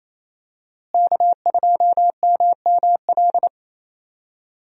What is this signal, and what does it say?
Morse code, heard as a keyed tone, about 28 wpm, 700 Hz. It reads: K2MML
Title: USA Call Signs - 5 Characters - Repeated 3x - 28wpm